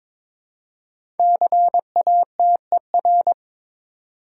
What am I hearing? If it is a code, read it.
Morse code, heard as a keyed tone, about 23 wpm, 700 Hz. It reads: CATER
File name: 5 Letter Words - First 3 Letters Given - 22wpm